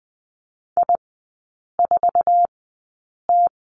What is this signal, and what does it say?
Morse code, heard as a keyed tone, about 20 wpm, spaced as for 8 wpm, 700 Hz. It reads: I4T
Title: 047 Lesson 9e - Number 4 - Three Characters 20wpm